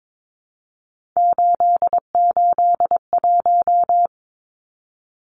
Morse code, heard as a keyed tone, about 22 wpm, 700 Hz. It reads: 881
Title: Numbers - 3-Digit 22wpm